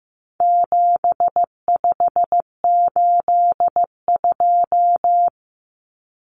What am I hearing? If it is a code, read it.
Morse code, heard as a keyed tone, about 15 wpm, 700 Hz. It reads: 7582
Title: Numbers - 4-Digit 15wpm